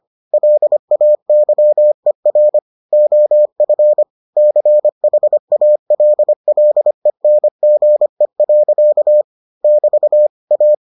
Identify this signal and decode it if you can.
Morse code, heard as a keyed tone, about 25 wpm, 600 Hz. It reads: LAYER OF CHALLENGE. <BT> A